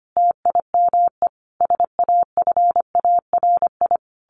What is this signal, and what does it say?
Morse code, heard as a keyed tone, about 25 wpm, 700 Hz. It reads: TIME SAFARI
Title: A Sound Of Thunder 25wpm